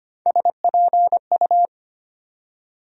Morse code, heard as a keyed tone, about 25 wpm, 700 Hz. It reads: SPU
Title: USA Call Sign Postfixes - 3 Letters 25wpm